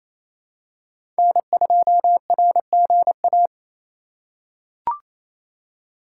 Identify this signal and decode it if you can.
Morse code, heard as a keyed tone, about 29 wpm, 700 Hz. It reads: N2RGA
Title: USA Call Signs - 5 Characters - 4x Character Spacing - 28wpm